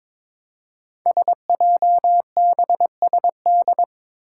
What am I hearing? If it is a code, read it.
Morse code, heard as a keyed tone, about 22 wpm, 700 Hz. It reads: SJBSD
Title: Five Challenging Letter-Number Combinations 22wpm